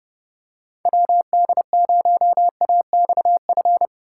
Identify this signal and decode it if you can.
Morse code, heard as a keyed tone, about 30 wpm, 700 Hz. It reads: WD0AXF